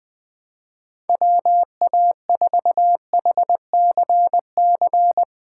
Morse code, heard as a keyed tone, about 20 wpm, 700 Hz. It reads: WA4HCC
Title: K1USN SST QSO Practice - 20wpm